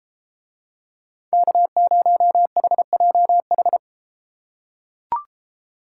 Morse code, heard as a keyed tone, about 30 wpm, 700 Hz. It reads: K0HJH